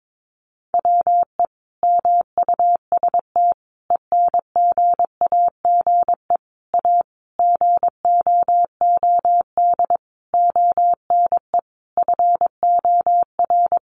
Morse code, heard as a keyed tone, about 22 wpm, 700 Hz. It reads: WE MUST ENGAGE A GOOD ONE FOR
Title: Sentences from Top 1500 Words 22wpm